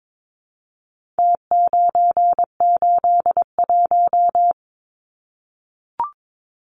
Morse code, heard as a keyed tone, about 22 wpm, 700 Hz. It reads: T981
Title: Numbers - Contest Style - 4-Digit 22wpm